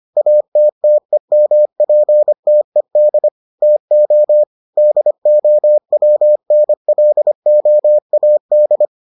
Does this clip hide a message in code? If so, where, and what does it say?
Yes: Morse code from 0.2 to 8.8 s, heard as a keyed tone, about 25 wpm, 600 Hz. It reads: ATTEMPTED TO DOWNLOAD